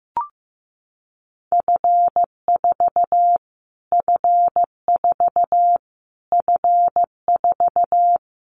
Morse code, heard as a keyed tone, about 14 wpm, 700 Hz. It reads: F4 F4 F4
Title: Callsign Prefixes for Top 25 DXCC - Repeated 3x - 15wpm